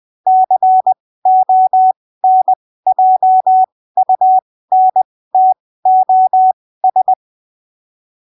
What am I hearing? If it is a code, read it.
Morse code, heard as a keyed tone, about 20 wpm, spaced as for 15 wpm, 750 Hz. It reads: CONJUNTOS
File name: arquivos / texto / texto 13 20ppm